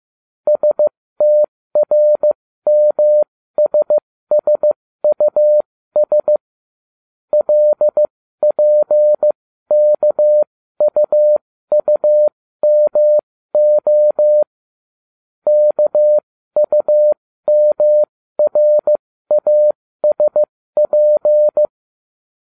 Morse code, heard as a keyed tone, about 15 wpm, spaced as for 13 wpm, 600 Hz. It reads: STRMSSUS LPKUUMO KUMRASP